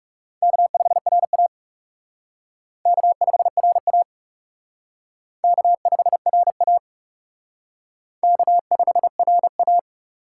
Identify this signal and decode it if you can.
Morse code, heard as a keyed tone, about 38 wpm, 700 Hz. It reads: K5RA K5RA K5RA K5RA
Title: Speed Racing - USA Call Signs 30wpm